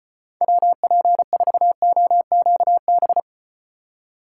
Morse code, heard as a keyed tone, about 36 wpm, 700 Hz. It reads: WP4OQB